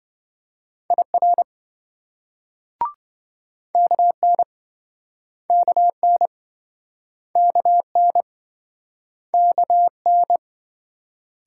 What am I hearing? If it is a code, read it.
Morse code, heard as a keyed tone, about 28 wpm, 700 Hz. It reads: IR KN KN KN KN